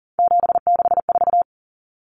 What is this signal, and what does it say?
Morse code, heard as a keyed tone, about 40 wpm, 700 Hz. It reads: 764
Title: Numbers - 3-Digit 40wpm